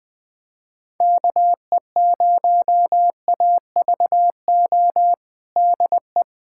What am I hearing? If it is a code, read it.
Morse code, heard as a keyed tone, about 20 wpm, 700 Hz. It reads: KE0AVO DE
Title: USA Callsign DE Callsign 20wpm